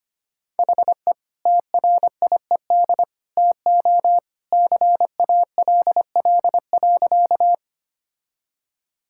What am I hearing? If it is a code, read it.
Morse code, heard as a keyed tone, about 25 wpm, 700 Hz. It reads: HE TRIED TO CALL.
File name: practice / 25wpm / Sentences with 2-Word Phrases - 25wpm